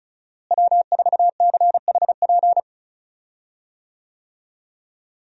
Morse code, heard as a keyed tone, about 36 wpm, 700 Hz. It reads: W4CHP
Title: USA Call Signs - 5 Characters - Repeated 2x - 35wpm